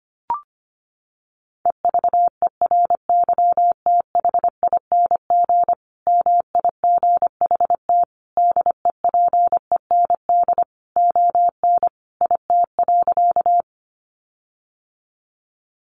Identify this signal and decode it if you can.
Morse code, heard as a keyed tone, about 25 wpm, 700 Hz. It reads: EVERYTHING MIGHT DEPEND ON IT.
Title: Sentences from Top 1300 Words 25wpm